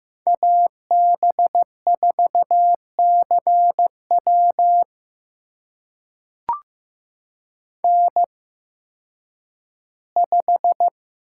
Morse code, heard as a keyed tone, about 15 wpm, spaced as for 6 wpm, 700 Hz. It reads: AB4C WN5
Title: USA Call Signs - 8x Character Spacing - 15wpm